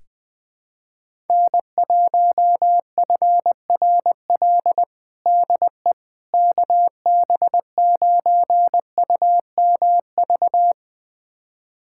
Morse code, heard as a keyed tone, about 21 wpm, 700 Hz. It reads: N1FRL DE KB9UMV
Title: USA Callsign DE Callsign 20wpm